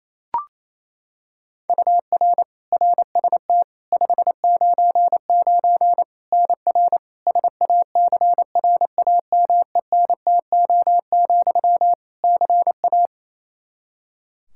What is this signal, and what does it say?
Morse code, heard as a keyed tone, about 28 wpm, 700 Hz. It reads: UR RST 599 NR SACRAMENTO, CA
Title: Practice QSO - Exchange 1 - Beginning - 28wpm